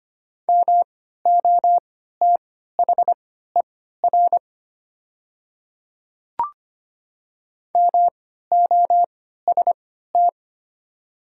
Morse code, heard as a keyed tone, about 25 wpm, spaced as for 14 wpm, 690 Hz. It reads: MOTHER MOST
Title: 3x Character Spacing - Top 300 Words - 25wpm